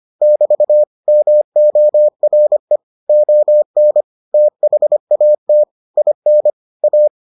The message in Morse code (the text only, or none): <BT> MORE ON THAT IN A